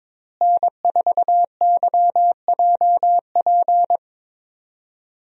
Morse code, heard as a keyed tone, about 21 wpm, 700 Hz. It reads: N4YJP
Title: USA Call Signs - 5 Characters - Repeated 3x - 22wpm